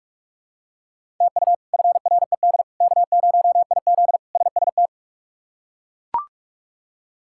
Morse code, heard as a keyed tone, about 41 wpm, 700 Hz. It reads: TU FRED K0IZ SST
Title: K1USN SST QSO Practice - 45wpm